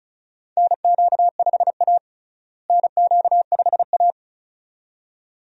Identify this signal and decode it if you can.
Morse code, heard as a keyed tone, about 36 wpm, 700 Hz. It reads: NQ5A NQ5A
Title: USA Call Signs - 4 Characters - Repeated 2x - 35wpm